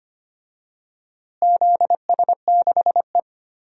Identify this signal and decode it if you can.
Morse code, heard as a keyed tone, about 25 wpm, 700 Hz. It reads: ZS6E
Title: DXCC-100 Call Signs - 4 Characters - Repeated 2x - 25wpm